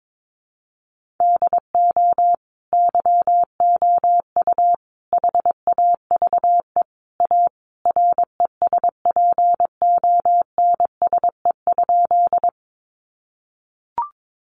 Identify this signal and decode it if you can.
Morse code, heard as a keyed tone, about 22 wpm, 700 Hz. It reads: DO YOU HAVE A RESPONSE?
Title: Sentences from Top 1400 Words 22wpm